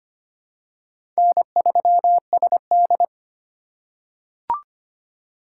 Morse code, heard as a keyed tone, about 25 wpm, 700 Hz. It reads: N3SD
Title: USA Call Signs - 3x Character Spacing - 25wpm